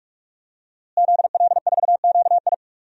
Morse code, heard as a keyed tone, about 41 wpm, 700 Hz. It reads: ZL4QI
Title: DXCC-100 Call Signs - 5 Characters - Repeated 3x - 45wpm